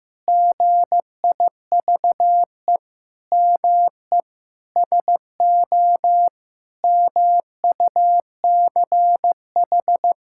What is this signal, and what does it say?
Morse code, heard as a keyed tone, about 15 wpm, 700 Hz. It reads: GIVE ME SO MUCH
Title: Sentences from Top 1600 Words 15wpm